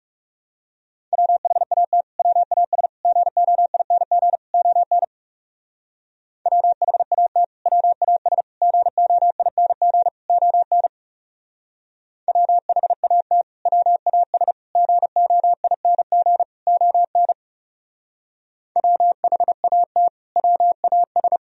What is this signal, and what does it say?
Morse code, heard as a keyed tone, about 39 wpm, 700 Hz. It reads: WHAT WAS GOING ON WHAT WAS GOING ON WHAT WAS GOING ON WHAT WAS